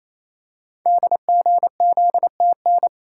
Morse code, heard as a keyed tone, about 29 wpm, 700 Hz. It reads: DGZTN
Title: Code Groups - 5 Letter:Numbers - 28wpm